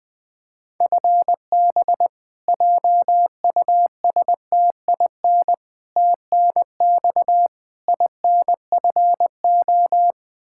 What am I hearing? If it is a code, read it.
Morse code, heard as a keyed tone, about 20 wpm, 700 Hz. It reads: FB JUSTIN TNX INFO
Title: Practice QSO - Exchange 3 - More Info - 20wpm